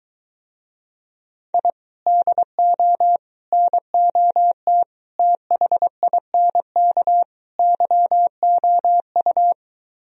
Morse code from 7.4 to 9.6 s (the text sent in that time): YOU